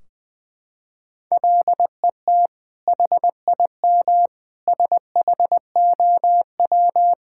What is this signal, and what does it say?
Morse code, heard as a keyed tone, about 20 wpm, 700 Hz. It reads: LET HIM SHOW
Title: Sentences from Top 1600 Words 20wpm